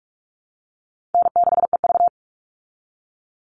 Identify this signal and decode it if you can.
Morse code, heard as a keyed tone, about 44 wpm, 700 Hz. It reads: N6EV
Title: Call Signs Repeated 2x - 45wpm